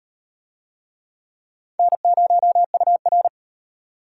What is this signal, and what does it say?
Morse code, heard as a keyed tone, about 36 wpm, 700 Hz. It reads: N0UR